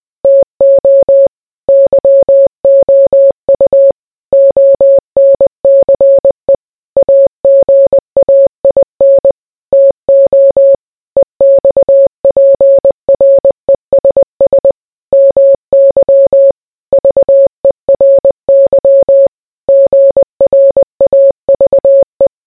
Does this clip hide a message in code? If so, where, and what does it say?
Yes: Morse code from 0.2 to 22.3 s, heard as a keyed tone, about 20 wpm, 550 Hz. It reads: TO YOU ONCE AGAIN TO EXPRESS MY VERY GRAVE